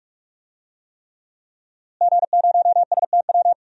Morse code, heard as a keyed tone, about 46 wpm, 700 Hz. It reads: G0STW
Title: DXCC-100 Call Signs - 5 Characters - Repeated 3x - 45wpm